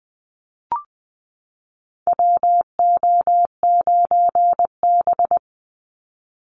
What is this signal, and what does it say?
Morse code, heard as a keyed tone, about 20 wpm, 700 Hz. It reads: WO9B
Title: Call Signs Repeated 3x - 20wpm